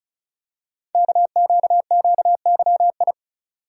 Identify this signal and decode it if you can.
Morse code, heard as a keyed tone, about 36 wpm, 700 Hz. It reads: KQQYI
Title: Code Groups - 5 Letter:Numbers - 35wpm